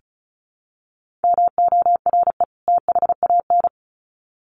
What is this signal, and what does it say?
Morse code, heard as a keyed tone, about 34 wpm, 700 Hz. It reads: MORE THAN
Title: Sets of 2 Words 35wpm